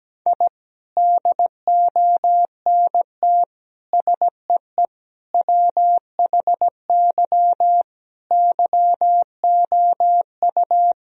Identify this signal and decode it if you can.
Morse code, heard as a keyed tone, about 17 wpm, 700 Hz. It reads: I DONT SEE WHY YOU
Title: Sentences from Top 1000 Words - Review Part 2 - 17wpm